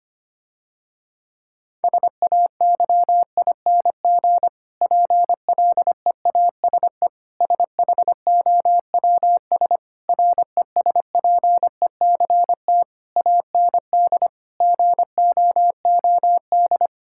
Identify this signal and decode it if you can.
Morse code, heard as a keyed tone, about 25 wpm, 700 Hz. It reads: SAYING PLEASE SHOWS RESPECT AND GOOD
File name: practice / 25wpm / Difficult Sentences with Hints - 25wpm